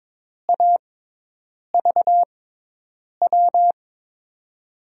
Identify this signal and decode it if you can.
Morse code, heard as a keyed tone, about 22 wpm, spaced as for 8 wpm, 700 Hz. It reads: AVW
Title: USA Call Signs - 6x Character Spacing - 22wpm